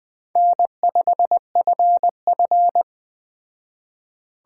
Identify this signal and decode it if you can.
Morse code, heard as a keyed tone, about 19 wpm, 700 Hz. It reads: N5FF